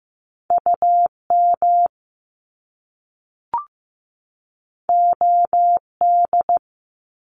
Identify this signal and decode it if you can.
Morse code, heard as a keyed tone, about 15 wpm, 700 Hz. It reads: UM OD
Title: Top Two Letter Combinations in Words 15wpm